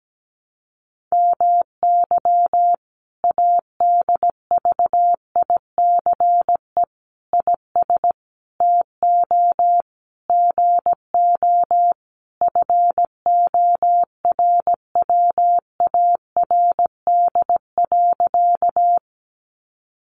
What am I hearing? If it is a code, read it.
Morse code, heard as a keyed tone, about 17 wpm, 700 Hz. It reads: MY ADVICE IS TO GO FORWARD.
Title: Sentences from Top 1300 Words 17wpm